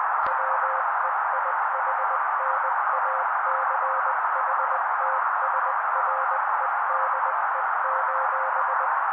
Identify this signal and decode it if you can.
Morse code, heard as a keyed tone, about 20 wpm, 530 Hz. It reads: WEIHNACHTSREDE:
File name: daily / 18wpm noise